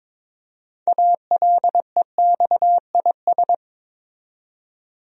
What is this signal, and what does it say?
Morse code, heard as a keyed tone, about 22 wpm, 700 Hz. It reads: ALEXIS
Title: USA Common Names - 22wpm